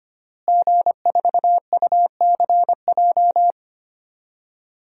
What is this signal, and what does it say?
Morse code, heard as a keyed tone, about 25 wpm, 700 Hz. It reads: G4UCJ
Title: DXCC-100 Call Signs - 5 Characters - Repeated 2x - 25wpm